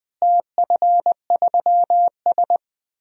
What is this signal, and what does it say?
Morse code, heard as a keyed tone, about 20 wpm, 700 Hz. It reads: TF3S
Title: DXCC-100 Call Signs - 4 Characters - Repeated 2x - 20wpm